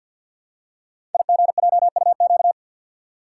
Morse code, heard as a keyed tone, about 48 wpm, 700 Hz. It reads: IGJUQ
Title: Code Groups - 5 Letters - 50wpm